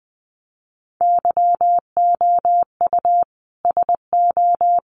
Morse code, heard as a keyed tone, about 20 wpm, 700 Hz. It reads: YOU SO
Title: Sets of 5 Words Encore2 20wpm